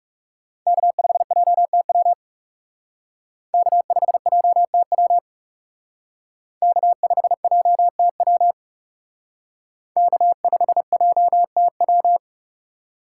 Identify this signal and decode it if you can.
Morse code, heard as a keyed tone, about 38 wpm, 700 Hz. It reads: K5JTW K5JTW K5JTW K5JTW